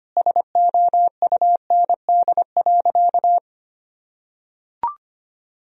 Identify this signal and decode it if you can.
Morse code, heard as a keyed tone, about 25 wpm, 700 Hz. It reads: SOUND.
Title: Sentences from Top 1000 Words - Review Part 1 - 25wpm